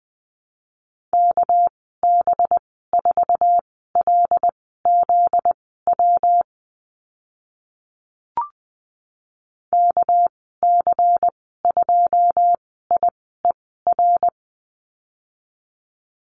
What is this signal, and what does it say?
Morse code, heard as a keyed tone, about 20 wpm, spaced as for 14 wpm, 700 Hz. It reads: KB4LZW KC2IER
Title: 245 Lesson 39g - Letter z - Call Signs 20wpm